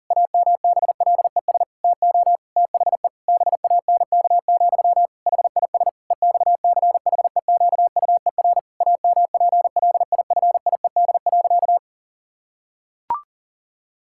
Morse code, heard as a keyed tone, about 40 wpm, 700 Hz. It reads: AMBLES TO THE BANK, HIS EXCHEQUER AMPLIFIED.
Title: Pangram Challenge v2 - 40wpm